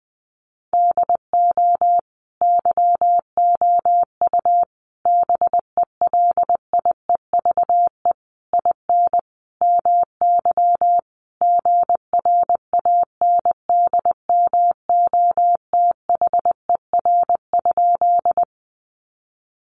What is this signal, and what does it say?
Morse code, heard as a keyed tone, about 19 wpm, 700 Hz. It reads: DO YOU BELIEVE IN MY GRANDMOTHER?
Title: Sentences from Top 1200 Words 20wpm